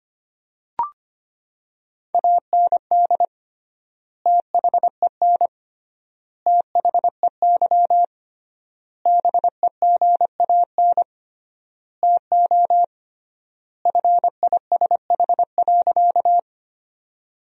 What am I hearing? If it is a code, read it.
Morse code, heard as a keyed tone, about 25 wpm, 700 Hz. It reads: AND THEN THEY BEGAN TO FISH.
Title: 3x Word-Spacing - Sentences from Top 300 Words - 25wpm